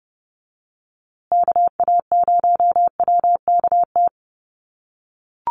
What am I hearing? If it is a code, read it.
Morse code, heard as a keyed tone, about 30 wpm, 700 Hz. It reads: KA0WKT
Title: USA Call Signs Repeated 2x 30wpm